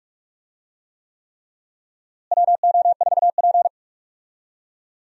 Morse code, heard as a keyed tone, about 46 wpm, 700 Hz. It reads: WO4P
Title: USA Call Signs - 4 Characters - Repeated 3x - 45wpm